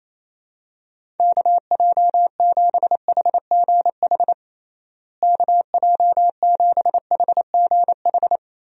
Morse code, heard as a keyed tone, about 28 wpm, 700 Hz. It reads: KJ7HGH KJ7HGH
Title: USA Call Signs - 6 Characters - Repeated 2x - 28wpm